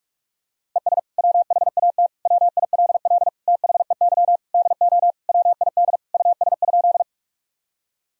Morse code, heard as a keyed tone, about 46 wpm, 700 Hz. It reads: ES WHAT WILL THEY DO WID US?
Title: Sentences from Top 200 Words and Abbreviations 45wpm